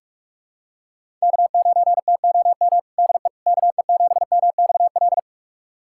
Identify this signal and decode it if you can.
Morse code, heard as a keyed tone, about 41 wpm, 700 Hz. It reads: K9TOM DE KE7MXL
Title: USA Callsign DE Callsign 45wpm